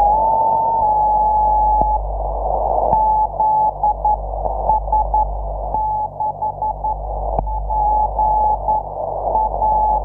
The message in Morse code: none